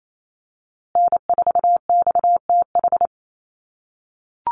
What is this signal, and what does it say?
Morse code, heard as a keyed tone, about 28 wpm, 700 Hz. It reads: N4XTH
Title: USA Call Signs Repeated 3x 28wpm